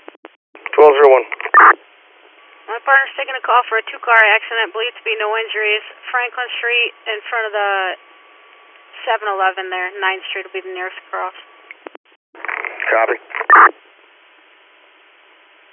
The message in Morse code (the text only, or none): none